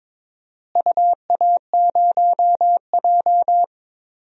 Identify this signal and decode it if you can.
Morse code, heard as a keyed tone, about 22 wpm, 700 Hz. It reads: UA0J
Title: DXCC-100 Call Signs - 4 Characters - Repeated 3x - 22wpm